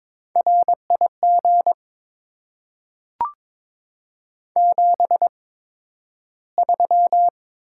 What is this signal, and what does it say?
Morse code, heard as a keyed tone, about 22 wpm, spaced as for 12 wpm, 700 Hz. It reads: RIG 73
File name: Top 20 QSO Elements - 8x Character Spacing - 22wpm